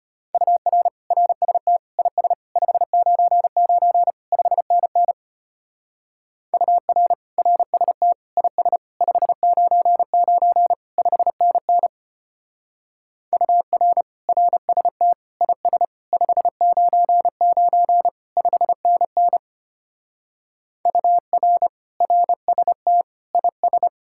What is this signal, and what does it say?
Morse code, heard as a keyed tone, about 31 wpm, 700 Hz. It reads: UR RST IS 599 5NN UR RST IS 599 5NN UR RST IS 599 5NN UR RST IS
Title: Speed-Racing Common QSO Phrases - 25wpm